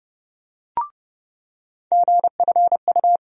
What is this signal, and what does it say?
Morse code, heard as a keyed tone, about 30 wpm, 700 Hz. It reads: GFU
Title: USA Call Sign Postfixes - 3 Letters 30wpm